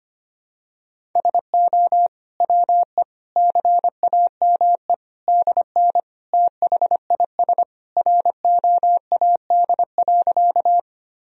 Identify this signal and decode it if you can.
Morse code, heard as a keyed tone, about 25 wpm, 700 Hz. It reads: SO WE CAME DN THIS ROAD.
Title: Sentences from Top 400 Words and Abbreviations 25wpm